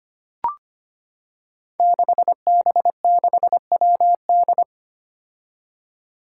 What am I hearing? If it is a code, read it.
Morse code, heard as a keyed tone, about 25 wpm, 700 Hz. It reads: 6B6WD
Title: Five Challenging Letter-Number Combinations 25wpm